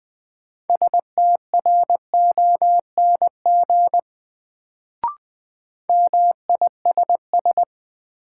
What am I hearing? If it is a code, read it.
Morse code, heard as a keyed tone, about 20 wpm, 700 Hz. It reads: STRONG MISS
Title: Sentences from Top 500 Words - New Word Drill - 20wpm